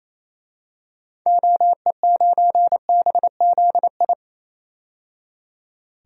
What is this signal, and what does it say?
Morse code, heard as a keyed tone, about 28 wpm, 700 Hz. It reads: OE9BZI
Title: DXCC-100 Call Signs - 6 Characters - 28wpm